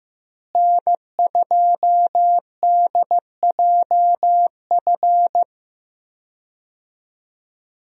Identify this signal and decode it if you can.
Morse code, heard as a keyed tone, about 15 wpm, 700 Hz. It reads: N2DJF
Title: USA Call Signs - 5 Characters - Repeated 3x - 15wpm